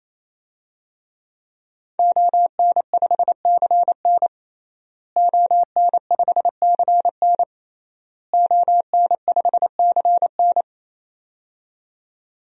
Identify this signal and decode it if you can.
Morse code, heard as a keyed tone, about 28 wpm, 700 Hz. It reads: ON5CN ON5CN ON5CN
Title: DXCC-100 Call Signs - 5 Characters - Repeated 3x - 28wpm